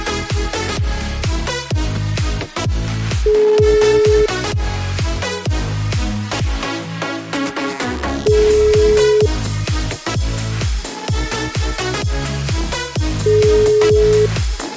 Ringback tone (Europe, 425 Hz), with noise.